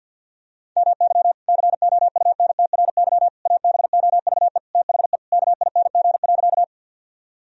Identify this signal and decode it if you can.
Morse code, heard as a keyed tone, about 52 wpm, 700 Hz. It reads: MY COUNTRY ABOVE THE KING.
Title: Sentences from Top 1000 Words - Review Part 2 - 50wpm